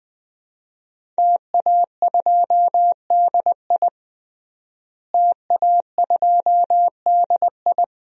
Morse code, heard as a keyed tone, about 20 wpm, 700 Hz. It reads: TA2DI TA2DI
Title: DXCC-100 Call Signs - 5 Characters - Repeated 2x - 20wpm